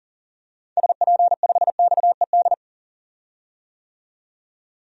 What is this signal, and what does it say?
Morse code, heard as a keyed tone, about 40 wpm, 700 Hz. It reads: SP5XED
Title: DXCC-100 Call Signs - 6 Characters - 40wpm